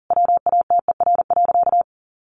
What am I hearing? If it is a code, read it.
Morse code, heard as a keyed tone, about 40 wpm, 700 Hz. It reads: WATER.
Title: Sentences from Top 1500 Words 40wpm